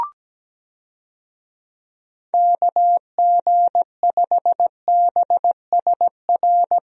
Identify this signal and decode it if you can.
Morse code, heard as a keyed tone, about 17 wpm, 700 Hz. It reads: KG5BSR
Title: USA Call Signs - 6 Characters - Repeated 3x - 17wpm